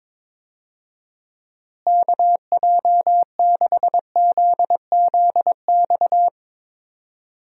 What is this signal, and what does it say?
Morse code, heard as a keyed tone, about 22 wpm, 700 Hz. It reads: KJ6ZZX
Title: USA Call Signs - 6 Characters - Repeated 2x - 22wpm